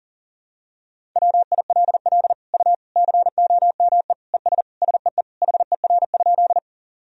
Morse code, heard as a keyed tone, about 40 wpm, 700 Hz. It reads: WILL U COME ES SEE HER?
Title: Sentences from Top 100 Words and Abbreviations 40wpm